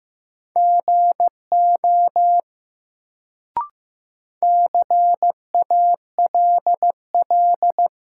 Morse code, heard as a keyed tone, about 15 wpm, 700 Hz. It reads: GO CALL
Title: Sentences from Top 100 Words - New Word Drill - 15wpm